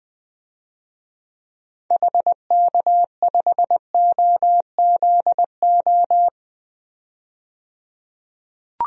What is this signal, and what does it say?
Morse code, heard as a keyed tone, about 20 wpm, 700 Hz. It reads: HK5OZO
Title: DXCC-100 Call Signs - 6 Characters - 20wpm